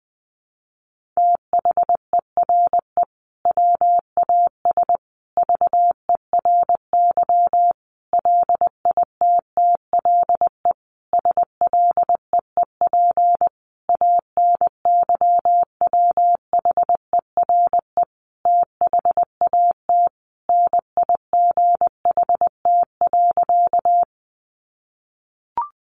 Morse code, heard as a keyed tone, about 19 wpm, 700 Hz. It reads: THERE WAS VERY LITTLE SLEEP ANYWHERE THAT NIGHT.
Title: Sentences from Top 1400 Words 20wpm